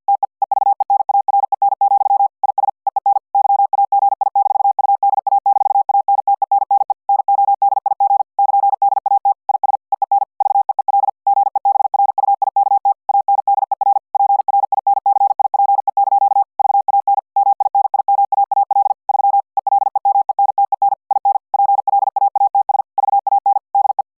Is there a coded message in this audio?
Yes: Morse code from 0.1 to 24.0 s, heard as a keyed tone, about 50 wpm, 800 Hz. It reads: TE EVENAREN, IS EEN YAGI-UDA-ANTENNE NODIG <KN>DAT IS EEN VEEL GEBRUIKT ANDER PRINCIPE) VAN MINIMAAL 3 ELEMENTEN IN PLAATS VAN DE